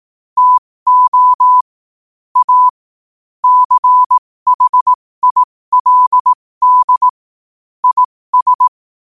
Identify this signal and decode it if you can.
Morse code, heard as a keyed tone, about 18 wpm, spaced as for 15 wpm, 1000 Hz. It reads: TO A CHILD IS
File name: qotdfiles / 15wpm